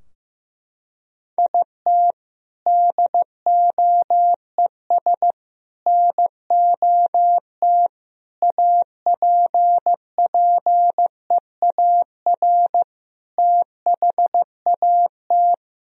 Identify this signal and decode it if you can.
Morse code, heard as a keyed tone, about 15 wpm, 700 Hz. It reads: IT DOES NOT APPEAR THAT
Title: Sentences from Top 1000 Words - Review Part 2 - 15wpm